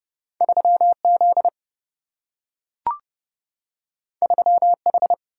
Morse code, heard as a keyed tone, about 30 wpm, 700 Hz. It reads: 3Z 3H
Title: Single-Letter Single-Number 30wpm